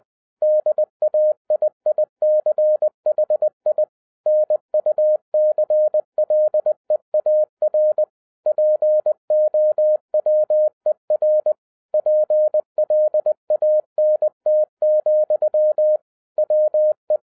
Morse code, heard as a keyed tone, about 20 wpm, 610 Hz. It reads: DAIICHI NUCLEAR POWER PLANT, WE
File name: 2011-10-mp3 / n03-TARO JR0QWW 20wpm